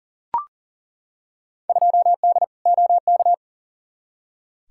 Morse code, heard as a keyed tone, about 40 wpm, 700 Hz. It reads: 2D OK
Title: Field Day Exchanges Encore - 40wpm